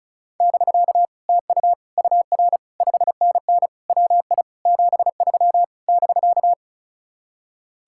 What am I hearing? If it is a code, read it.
Morse code, heard as a keyed tone, about 42 wpm, 700 Hz. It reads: <BK> TU UR 5NN WI 73 <BK>
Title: Speed-Racing Common QSO Phrases - 35wpm